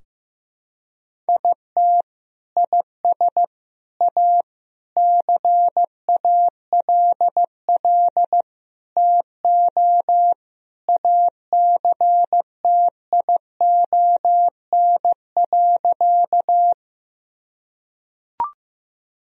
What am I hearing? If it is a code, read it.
Morse code, heard as a keyed tone, about 16 wpm, 700 Hz. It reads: IT IS A CALL TO ACTION.
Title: Sentences from Top 1000 Words - Review Part 4 - 15wpm